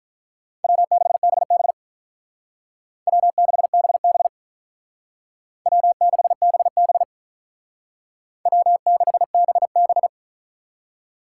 Morse code, heard as a keyed tone, about 45 wpm, 700 Hz. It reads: W6BB W6BB W6BB W6BB